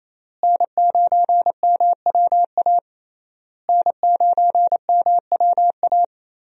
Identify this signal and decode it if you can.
Morse code, heard as a keyed tone, about 28 wpm, 700 Hz. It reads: N9MWA N9MWA
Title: USA Call Signs - 5 Characters - Repeated 3x - 28wpm